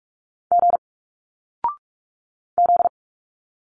Morse code, heard as a keyed tone, about 46 wpm, 700 Hz. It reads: G Z
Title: Single Letters 45wpm